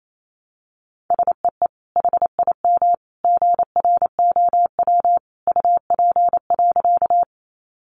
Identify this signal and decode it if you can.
Morse code, heard as a keyed tone, about 28 wpm, 700 Hz. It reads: SEE HIM GROW UP.
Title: Sentences from Top 300 Words 28wpm